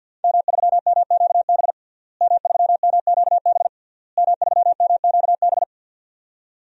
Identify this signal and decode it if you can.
Morse code, heard as a keyed tone, about 52 wpm, 700 Hz. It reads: M3MQB M3MQB M3MQB
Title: DXCC-100 Call Signs - 5 Characters - Repeated 3x - 50wpm